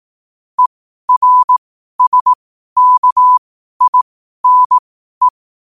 Morse code, heard as a keyed tone, about 18 wpm, spaced as for 12 wpm, 1000 Hz. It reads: ERSKINE
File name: qotdfiles / 12wpm